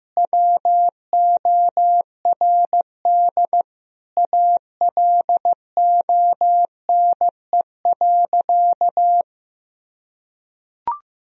Morse code, heard as a keyed tone, about 16 wpm, 700 Hz. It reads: WORD ALONE.
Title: Sentences from Top 1000 Words - Review Part 2 - 15wpm